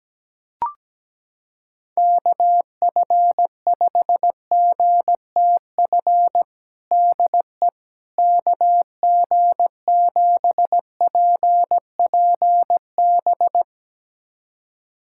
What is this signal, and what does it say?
Morse code, heard as a keyed tone, about 17 wpm, 700 Hz. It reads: KF5GTF DE KG7PPB
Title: USA Callsign DE Callsign 17wpm